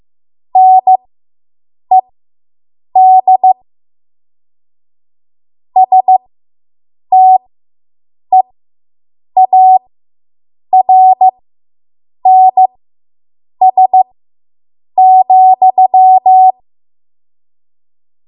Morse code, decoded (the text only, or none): NED STEARNS,